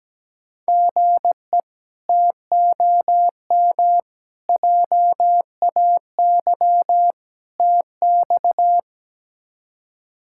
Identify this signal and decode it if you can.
Morse code, heard as a keyed tone, about 17 wpm, 700 Hz. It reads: GE TOM JAY TX
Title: K1USN SST QSO Practice - 17wpm